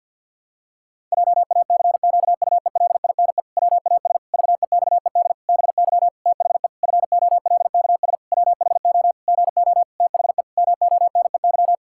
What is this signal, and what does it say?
Morse code, heard as a keyed tone, about 50 wpm, 700 Hz. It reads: JACQUELINE WAS VEXED BY THE FOLKS WHO GO THE MONEY
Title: Pangram Challenge v2 - 50wpm